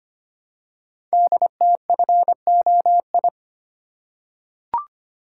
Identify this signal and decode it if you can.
Morse code, heard as a keyed tone, about 25 wpm, 700 Hz. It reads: DTFOI
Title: Code Groups - 5 Letters - 25wpm